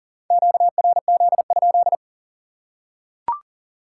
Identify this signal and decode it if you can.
Morse code, heard as a keyed tone, about 40 wpm, 700 Hz. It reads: QRZ?
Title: Top 180 QSO Elements 40wpm